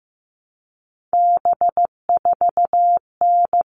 Morse code, heard as a keyed tone, about 15 wpm, 700 Hz. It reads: B4N
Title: All Cut Numbers - 3-Digit 15wpm